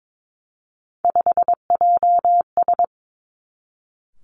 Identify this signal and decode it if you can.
Morse code, heard as a keyed tone, about 22 wpm, 700 Hz. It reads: 5JS
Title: Three Challenging Letter-Number Combinations 22wpm